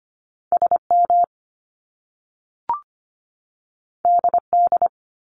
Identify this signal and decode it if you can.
Morse code, heard as a keyed tone, about 25 wpm, 700 Hz. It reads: SM DD
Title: Top Two Letter Combinations in Words 25wpm